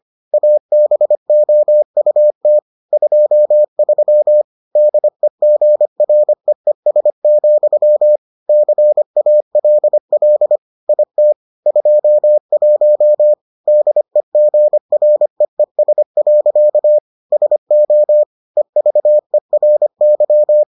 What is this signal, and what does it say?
Morse code, heard as a keyed tone, about 25 wpm, 600 Hz. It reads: ABOUT 23 DEGREES, CALL IT 21 DEGREES. SO EVERY